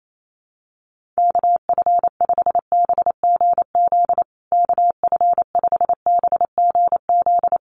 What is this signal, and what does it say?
Morse code, heard as a keyed tone, about 28 wpm, 690 Hz. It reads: KF5BGZ KF5BGZ
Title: USA Call Signs Repeated 3x 28wpm